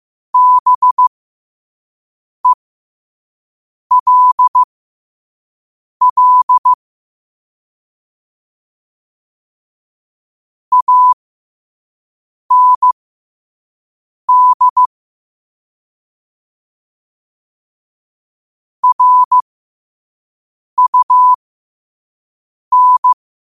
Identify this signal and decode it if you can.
Morse code, heard as a keyed tone, about 15 wpm, spaced as for 5 wpm, 1000 Hz. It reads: BELL AND RUN
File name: qotdfiles / 5wpm